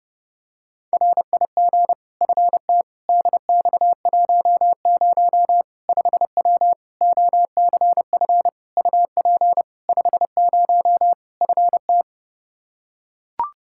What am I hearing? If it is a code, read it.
Morse code, heard as a keyed tone, about 32 wpm, 700 Hz. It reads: RIG FT DX10 5W OCF UP 50 FT
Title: Practice QSO - Exchange 2 - Info - 30wpm